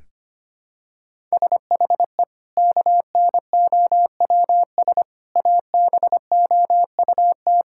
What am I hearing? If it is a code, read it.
Morse code, heard as a keyed tone, about 25 wpm, 700 Hz. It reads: SHE KNOWS ABOUT
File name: practice / 25wpm / Sentences with 2-Word Phrases - 25wpm